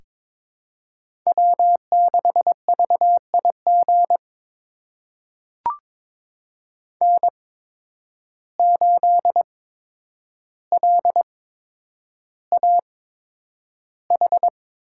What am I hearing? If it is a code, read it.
Morse code, heard as a keyed tone, about 22 wpm, spaced as for 9 wpm, 700 Hz. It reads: W6VIG N8LAH
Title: USA Call Signs - 5 Characters - 8x Character Spacing - 22wpm